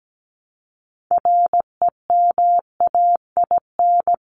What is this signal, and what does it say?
Morse code, heard as a keyed tone, about 17 wpm, 700 Hz. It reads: REMAIN
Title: Top 1001-1250 Words 17wpm